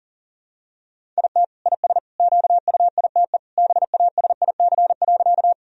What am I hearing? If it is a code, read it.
Morse code, heard as a keyed tone, about 40 wpm, 700 Hz. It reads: IT IS QUITE BASIC.
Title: Sentences from Top 1000 Words - Review Part 3 - 40wpm